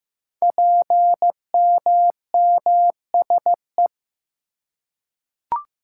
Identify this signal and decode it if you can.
Morse code, heard as a keyed tone, about 15 wpm, 700 Hz. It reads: PMMSE
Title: Code Groups - 5 Letters - 15wpm